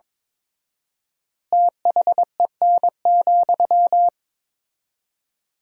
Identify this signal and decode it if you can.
Morse code, heard as a keyed tone, about 22 wpm, 700 Hz. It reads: THEN,
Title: 4x Word-Spacing - Sentences from Top 200 Words - 22wpm